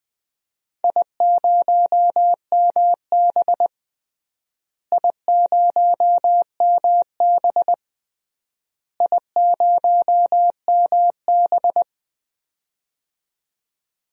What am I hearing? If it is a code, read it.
Morse code, heard as a keyed tone, about 20 wpm, 700 Hz. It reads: I0MB I0MB I0MB
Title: DXCC-100 Call Signs - 4 Characters - Repeated 3x - 20wpm